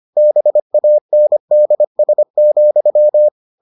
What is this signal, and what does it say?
Morse code, heard as a keyed tone, about 25 wpm, 600 Hz. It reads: BANDS,